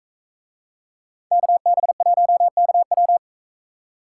Morse code, heard as a keyed tone, about 42 wpm, 700 Hz. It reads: KB1KW